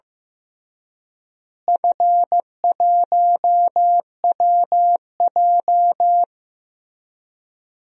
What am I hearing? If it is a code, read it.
Morse code, heard as a keyed tone, about 15 wpm, 700 Hz. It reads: F1WJ
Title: DXCC-100 Call Signs - 4 Characters - Repeated 3x - 15wpm